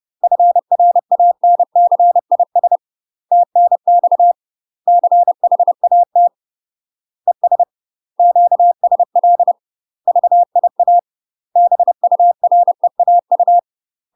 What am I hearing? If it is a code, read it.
Morse code, heard as a keyed tone, about 30 wpm, 700 Hz. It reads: FRANCIS TNX CHAT ES QSL VIA BUREAU
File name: UK MASTER QSOs TRAINING / 30WPM / UK MASTER QSOs 0027